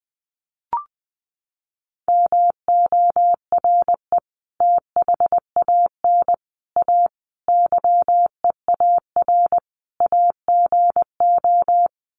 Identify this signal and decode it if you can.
Morse code, heard as a keyed tone, about 20 wpm, 700 Hz. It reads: MORE THAN A YEAR AGO
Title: Sets of 5 Words Encore2 20wpm